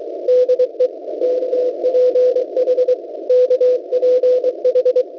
Morse code, heard as a keyed tone, about 23 wpm, 500 Hz, with radio noise. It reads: DE KPH KPH
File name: WebSDR records / w7pua-2.ddns.net 17016.80 cw